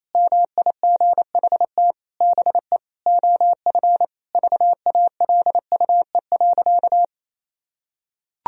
Morse code, heard as a keyed tone, about 27 wpm, 700 Hz. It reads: MIGHT BE OF VALUE.
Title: Sentences from Top 1000 Words - Review Part 3 - 28wpm